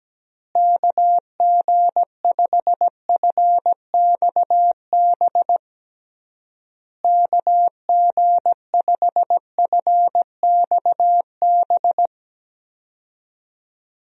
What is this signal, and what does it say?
Morse code, heard as a keyed tone, about 17 wpm, 700 Hz. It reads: KG5FXB KG5FXB
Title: USA Call Signs - 6 Characters - Repeated 3x - 17wpm